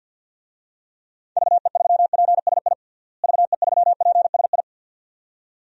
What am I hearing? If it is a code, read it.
Morse code, heard as a keyed tone, about 50 wpm, 700 Hz. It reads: VE3PSI VE3PSI
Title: DXCC-100 Call Signs - 6 Characters - Repeated 2x - 50wpm